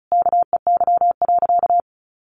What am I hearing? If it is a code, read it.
Morse code, heard as a keyed tone, about 34 wpm, 700 Hz. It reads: KEY.
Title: Sentences from Top 1400 Words 35wpm